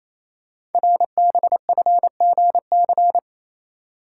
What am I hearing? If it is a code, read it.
Morse code, heard as a keyed tone, about 29 wpm, 700 Hz. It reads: RBFGC
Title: Code Groups - 5 Letters - 28wpm